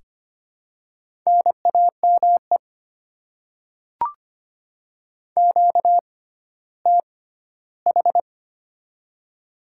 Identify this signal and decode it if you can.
Morse code, heard as a keyed tone, about 25 wpm, spaced as for 14 wpm, 700 Hz. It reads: NAME QTH
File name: Top 20 QSO Elements - 6x Character Spacing - 25wpm